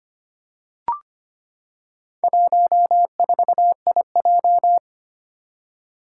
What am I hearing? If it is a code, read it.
Morse code, heard as a keyed tone, about 25 wpm, 700 Hz. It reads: 14IJ
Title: Four Challenging Letter-Number Combinations 25wpm